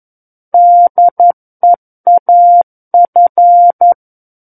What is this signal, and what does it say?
Morse code, heard as a keyed tone, about 11 wpm, 700 Hz. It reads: DEAF
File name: Book One / 02-Chapter 2 - 11 WPM